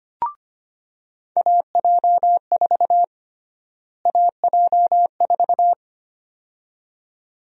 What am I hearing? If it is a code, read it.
Morse code, heard as a keyed tone, about 25 wpm, 700 Hz. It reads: AJ4 AJ4
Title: USA Call Sign Prefixes - Repeated 2x - 25wpm